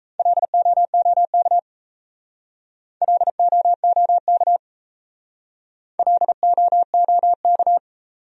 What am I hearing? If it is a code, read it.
Morse code, heard as a keyed tone, about 37 wpm, 700 Hz. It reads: LOOK LOOK LOOK